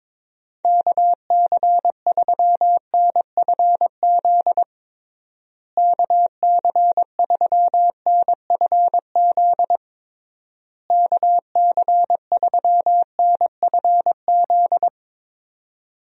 Morse code, heard as a keyed tone, about 22 wpm, 700 Hz. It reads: KC3NFZ KC3NFZ KC3NFZ
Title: USA Call Signs - 6 Characters - Repeated 3x - 22wpm